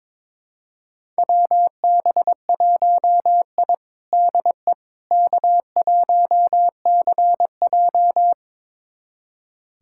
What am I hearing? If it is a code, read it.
Morse code, heard as a keyed tone, about 22 wpm, 700 Hz. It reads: WB1I DE K1CJ
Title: USA Callsign DE Callsign - 4 Characters - 22wpm